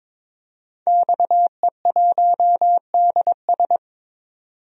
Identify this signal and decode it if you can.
Morse code, heard as a keyed tone, about 22 wpm, 700 Hz. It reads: XE1DS
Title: DXCC-100 Call Signs - 5 Characters - 22wpm